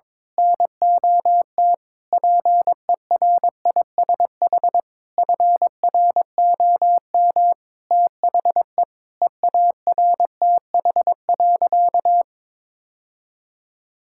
Morse code, heard as a keyed tone, about 22 wpm, 700 Hz. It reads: NOT PERISH FROM THE EARTH.
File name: Sentences with 9-Word Phrases - 22wpm